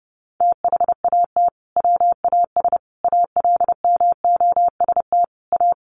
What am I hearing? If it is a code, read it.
Morse code, heard as a keyed tone, about 30 wpm, 700 Hz. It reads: THAT WAS ALMOST A
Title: Sentences from Top 1400 Words 30wpm